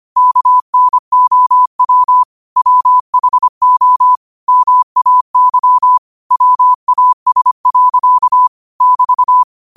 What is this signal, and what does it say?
Morse code, heard as a keyed tone, about 25 wpm, 1000 Hz. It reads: KNOW WHO MAY WAS. <BT>